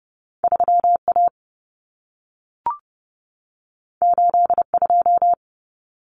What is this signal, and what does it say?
Morse code, heard as a keyed tone, about 30 wpm, 700 Hz. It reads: 3A 82
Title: Common Cut Numbers - 2-Digit 30wpm